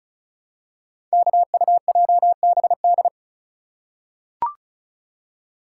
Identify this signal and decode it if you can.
Morse code, heard as a keyed tone, about 36 wpm, 700 Hz. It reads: KUJBD
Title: Code Groups - 5 Letter:Numbers - 35wpm